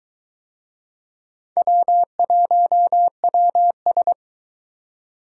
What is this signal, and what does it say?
Morse code, heard as a keyed tone, about 22 wpm, 700 Hz. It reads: W1WS